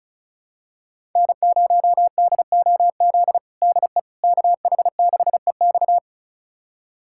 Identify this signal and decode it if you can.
Morse code, heard as a keyed tone, about 36 wpm, 690 Hz. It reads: N0DOZ DE KH6EX
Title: USA Callsign DE Callsign - 5 Characters - 35wpm